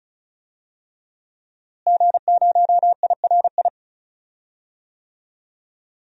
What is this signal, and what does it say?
Morse code, heard as a keyed tone, about 36 wpm, 700 Hz. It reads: G0IRI
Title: DXCC-100 Call Signs - 5 Characters - Repeated 3x - 35wpm